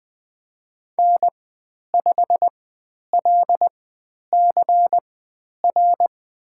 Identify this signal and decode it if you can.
Morse code, heard as a keyed tone, about 20 wpm, spaced as for 10 wpm, 700 Hz. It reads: N5LCR
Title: 102 Lesson 17g - Letter u - Call Signs 20wpm